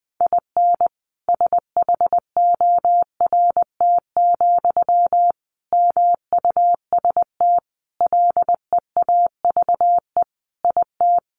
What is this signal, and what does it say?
Morse code, heard as a keyed tone, about 20 wpm, 700 Hz. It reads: IN SHORT, MUST LEAVE IT
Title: Sentences from Top 1000 Words 20wpm